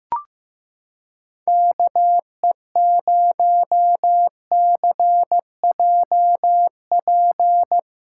Morse code, heard as a keyed tone, about 16 wpm, 700 Hz. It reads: KE0CJP
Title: USA Callsign DE Callsign - 6 Characters - 15wpm